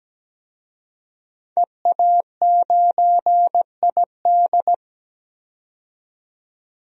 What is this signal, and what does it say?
Morse code, heard as a keyed tone, about 17 wpm, 700 Hz. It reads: EA9ID
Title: DXCC-100 Call Signs - 5 Characters - 17wpm